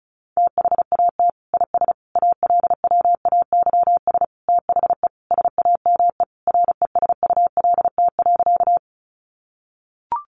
Morse code, heard as a keyed tone, about 34 wpm, 700 Hz. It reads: THAT IS ALWAYS THE SAME RESULT.
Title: Sentences from Top 700 Words 35wpm